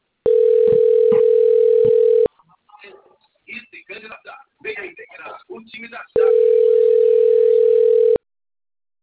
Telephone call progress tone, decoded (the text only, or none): ringback tone